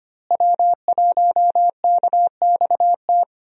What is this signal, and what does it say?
Morse code, heard as a keyed tone, about 25 wpm, 700 Hz. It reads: W1KXT
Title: USA Call Signs - 5 Characters - Repeated 2x - 25wpm